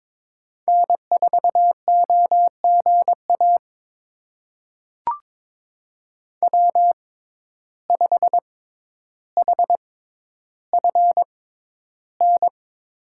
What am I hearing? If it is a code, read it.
Morse code, heard as a keyed tone, about 22 wpm, spaced as for 11 wpm, 700 Hz. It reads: N4OGA W5HFN
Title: USA Call Signs - 5 Characters - 6x Character Spacing - 22wpm